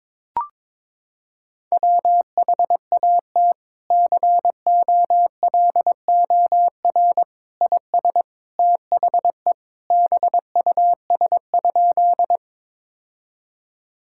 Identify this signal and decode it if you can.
Morse code, heard as a keyed tone, about 22 wpm, 700 Hz. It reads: WHAT COLOR IS THE BUS?
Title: Sentences from Top 1000 Words - Review Part 6 - 22wpm